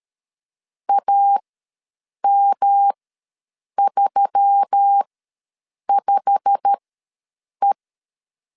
Morse code, heard as a keyed tone, about 13 wpm, spaced as for 7 wpm, 800 Hz. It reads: AM35E
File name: LEZIONI / 8 YJ45 / 8 YJ45 ws 12wpm 3146